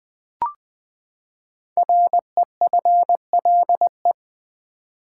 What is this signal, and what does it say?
Morse code, heard as a keyed tone, about 20 wpm, 700 Hz. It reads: REFLE
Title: Build A Word - 5 Letters At A Time - 10-Letter Words - 20wpm